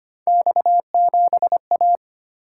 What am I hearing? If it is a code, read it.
Morse code, heard as a keyed tone, about 25 wpm, 700 Hz. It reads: X7A
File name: Three Letter-Number Combinations - 25wpm